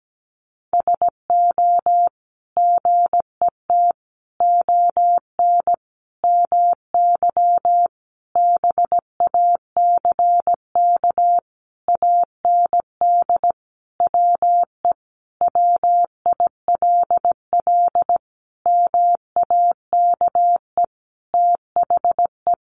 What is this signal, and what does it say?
Morse code, heard as a keyed tone, about 17 wpm, 700 Hz. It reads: SO GET ON MY BACK AND WE WILL MAKE THE
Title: Sentences from Top 1000 Words 17wpm